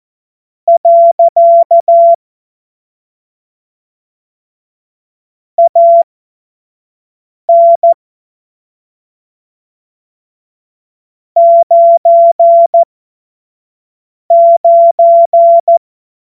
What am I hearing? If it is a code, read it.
Morse code, heard as a keyed tone, about 14 wpm, spaced as for 5 wpm, 680 Hz. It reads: . AN 99
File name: Morsekurs / Witze / Witz 26. April 2022 25 F70